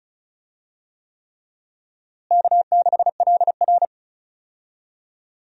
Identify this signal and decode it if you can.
Morse code, heard as a keyed tone, about 36 wpm, 700 Hz. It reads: K6LR
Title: USA Call Signs - 4 Characters - Repeated 3x - 35wpm